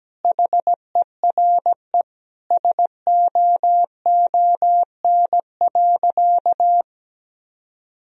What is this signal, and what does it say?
Morse code, heard as a keyed tone, about 17 wpm, 700 Hz. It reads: HERE SOON.
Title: Sentences from Top 1000 Words - Review Part 3 - 17wpm